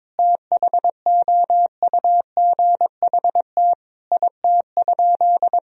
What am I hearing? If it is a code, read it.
Morse code, heard as a keyed tone, about 22 wpm, 700 Hz. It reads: THOUGHT IT?
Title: Sentences from Top 300 Words and Abbreviations 22wpm